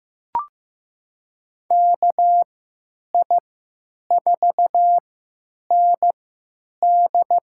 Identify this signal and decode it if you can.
Morse code, heard as a keyed tone, about 15 wpm, spaced as for 9 wpm, 700 Hz. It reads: KI4ND
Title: USA Call Signs - 3x Character Spacing - 15wpm